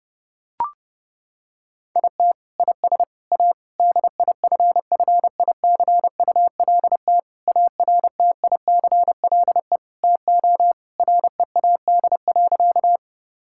Morse code, heard as a keyed tone, about 30 wpm, 700 Hz. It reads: IT IS A DIFFICULT ARTICLE TO READ.
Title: Sentences from Top 1000 Words - Review Part 5 - 30wpm